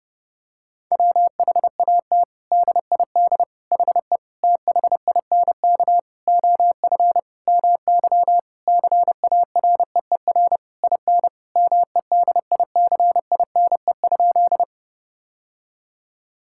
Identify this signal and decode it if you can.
Morse code, heard as a keyed tone, about 30 wpm, 700 Hz. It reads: WHAT DID HE THINK OF MY CAREER IN MEDICINE?
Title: Sentences from Top 1000 Words - Review Part 6 - 30wpm